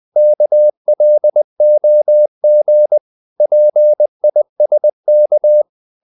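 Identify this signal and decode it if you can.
Morse code, heard as a keyed tone, about 20 wpm, 600 Hz. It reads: KLOG PISK
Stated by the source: Morse code practice of short Danish words at 20 wpm, from CW med Gnister.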